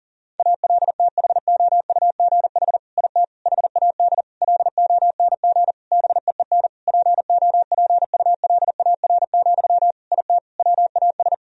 Morse code, heard as a keyed tone, about 40 wpm, 700 Hz. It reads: ALTHOUGH IT HAD LONG BEEN POPULAR, IT WAS
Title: Sentences from Top 1000 Words - Review Part 3 - 40wpm